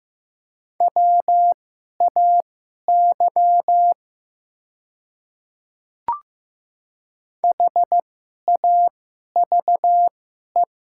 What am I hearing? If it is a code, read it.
Morse code, heard as a keyed tone, about 15 wpm, spaced as for 11 wpm, 700 Hz. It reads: WAY HAVE
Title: 2x Character Spacing - Top 100 Words - 15wpm